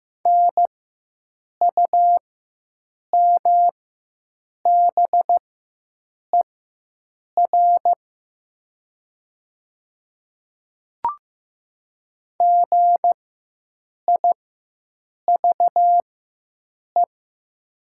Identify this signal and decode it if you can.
Morse code, heard as a keyed tone, about 15 wpm, spaced as for 7 wpm, 700 Hz. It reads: NUMBER GIVE
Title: 4x Character Spacing - Top 200 Words - 15wpm